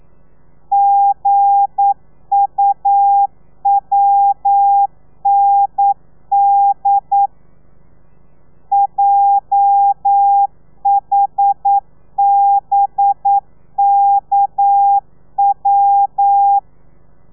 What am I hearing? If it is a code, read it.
Morse code, heard as a keyed tone, about 9 wpm, 800 Hz. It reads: GUWND JHBKW
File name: telegrafia / i2tfj / Lez8 40